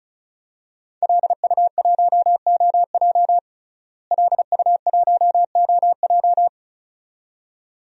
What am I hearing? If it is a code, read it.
Morse code, heard as a keyed tone, about 36 wpm, 700 Hz. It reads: LU1OJ LU1OJ
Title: DXCC-100 Call Signs - 5 Characters - Repeated 2x - 35wpm